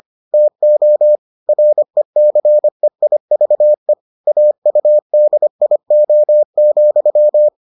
Morse code, heard as a keyed tone, about 25 wpm, 600 Hz. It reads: TO RECEIVE AUDIO,